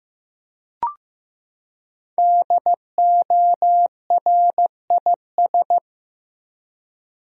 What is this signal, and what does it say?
Morse code, heard as a keyed tone, about 15 wpm, 700 Hz. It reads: DORIS
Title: USA Common Names - Max 5 Letters - 15wpm